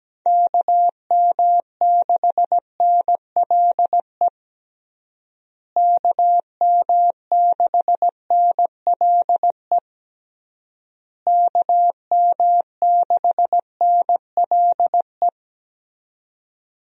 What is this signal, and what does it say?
Morse code, heard as a keyed tone, about 17 wpm, 700 Hz. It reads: KM6NLE KM6NLE KM6NLE
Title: USA Call Signs - 6 Characters - Repeated 3x - 17wpm